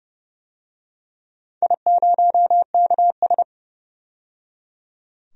Morse code, heard as a keyed tone, about 30 wpm, 700 Hz. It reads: I0KS
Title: DXCC-100 Call Signs - 4 Characters - 30wpm